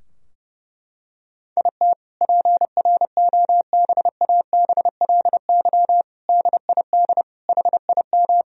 Morse code, heard as a keyed tone, about 30 wpm, 700 Hz. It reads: IT PROBABLY DID HIM
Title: Sentences from Top 1000 Words - Review Part 4 - 30wpm